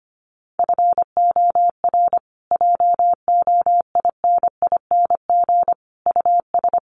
Morse code, heard as a keyed tone, about 25 wpm, 700 Hz. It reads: FOR JOINING US
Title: Sets of 5 Words Encore 25wpm